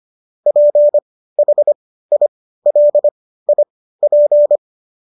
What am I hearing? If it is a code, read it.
Morse code, heard as a keyed tone, about 25 wpm, spaced as for 15 wpm, 590 Hz. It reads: PHILIP